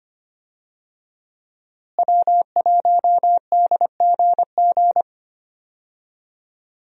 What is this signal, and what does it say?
Morse code, heard as a keyed tone, about 25 wpm, 700 Hz. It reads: W1DGG
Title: USA Call Signs - 5 Characters - Repeated 3x - 25wpm